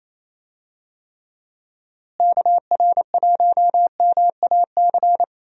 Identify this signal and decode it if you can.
Morse code, heard as a keyed tone, about 28 wpm, 700 Hz. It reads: KR1MAC
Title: USA Call Signs - 6 Characters - Repeated 3x - 28wpm